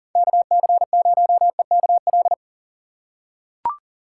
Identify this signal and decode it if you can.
Morse code, heard as a keyed tone, about 37 wpm, 700 Hz. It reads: KC0EKL
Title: USA Call Signs - 6 Characters - 6x Character Spacing - 40wpm